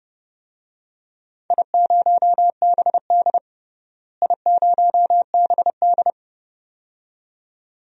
Morse code, heard as a keyed tone, about 30 wpm, 700 Hz. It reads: I0BD I0BD
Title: DXCC-100 Call Signs - 4 Characters - Repeated 2x - 30wpm